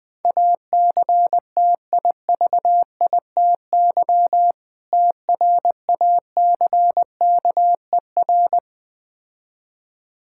Morse code, heard as a keyed tone, about 20 wpm, 690 Hz. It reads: ACTIVITY TRACKER
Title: Difficult Words - Speed-Racing with Hints - 20wpm